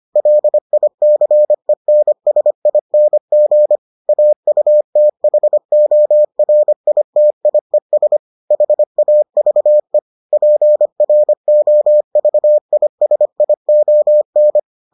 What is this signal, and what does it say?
Morse code, heard as a keyed tone, about 25 wpm, 600 Hz. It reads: LICENSING AUTHORITIES HAVE PROVISION